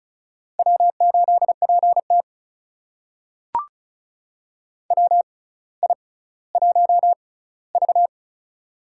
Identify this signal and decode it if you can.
Morse code, heard as a keyed tone, about 36 wpm, spaced as for 18 wpm, 700 Hz. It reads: W8PT WI1V
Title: USA Call Signs - 4 Characters - 6x Character Spacing - 35wpm